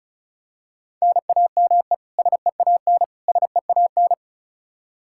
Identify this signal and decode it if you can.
Morse code, heard as a keyed tone, about 36 wpm, 700 Hz. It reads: NAME SEAN SEAN
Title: Practice QSO - Exchange 1 - Beginning - 35wpm